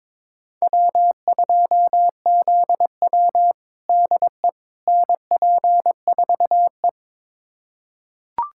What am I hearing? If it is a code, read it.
Morse code, heard as a keyed tone, about 22 wpm, 700 Hz. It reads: W2ZW DE NP4E
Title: USA Callsign DE Callsign - 4 Characters - 22wpm